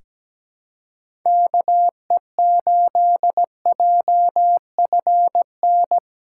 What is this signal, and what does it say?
Morse code, heard as a keyed tone, about 17 wpm, 700 Hz. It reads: KE8JFN
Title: USA Callsign DE Callsign - 6 Characters - 17wpm